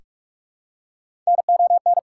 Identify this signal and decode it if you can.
Morse code, heard as a keyed tone, about 43 wpm, 690 Hz. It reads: NON
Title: Prefixes and Suffixes 45wpm